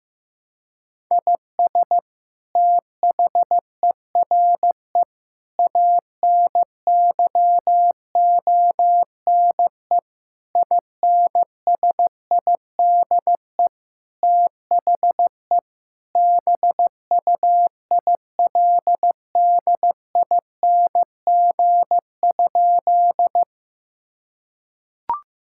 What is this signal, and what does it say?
Morse code, heard as a keyed tone, about 15 wpm, 700 Hz. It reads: IS THERE ANYONE INSIDE THE BUILDING?
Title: Sentences from Top 1000 Words - Review Part 6 - 15wpm